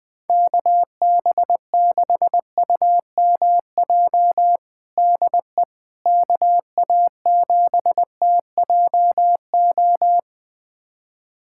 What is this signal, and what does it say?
Morse code, heard as a keyed tone, about 20 wpm, 700 Hz. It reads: KB6UMJ DE KA7TJO
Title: USA Callsign DE Callsign - 6 Characters - 20wpm